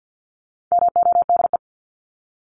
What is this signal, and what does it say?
Morse code, heard as a keyed tone, about 50 wpm, 700 Hz. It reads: MODE
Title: Top 1751-2000 Words 50wpm